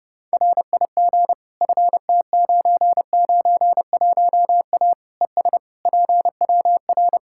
Morse code, heard as a keyed tone, about 32 wpm, 700 Hz. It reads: RIG FT991A ES PWR
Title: Practice QSO - Exchange 2 - Info - 30wpm